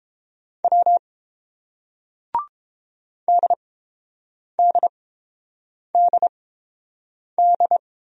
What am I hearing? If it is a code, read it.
Morse code, heard as a keyed tone, about 31 wpm, spaced as for 7 wpm, 690 Hz. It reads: W DDDD